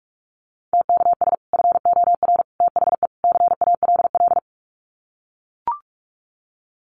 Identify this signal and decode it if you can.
Morse code, heard as a keyed tone, about 44 wpm, 700 Hz. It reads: TKS FOR THE CALL
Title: Common QSO Phrases 45wpm